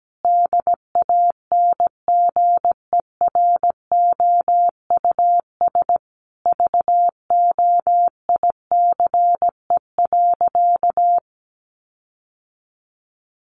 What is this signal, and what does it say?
Morse code, heard as a keyed tone, about 17 wpm, 700 Hz. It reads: DANGEROUS VOICE.
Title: Sentences from Top 1500 Words 17wpm